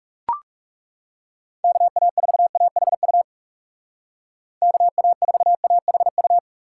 Morse code, heard as a keyed tone, about 40 wpm, 700 Hz. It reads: KA4AHU KA4AHU